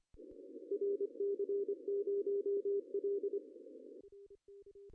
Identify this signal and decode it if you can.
Morse code, heard as a keyed tone, about 27 wpm, 410 Hz. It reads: RC0L
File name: CW / cqww-cw-k5tr